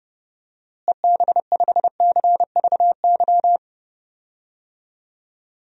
Morse code, heard as a keyed tone, about 31 wpm, 700 Hz. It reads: EB5CVY
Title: DXCC-100 Call Signs - 6 Characters - Repeated 2x - 30wpm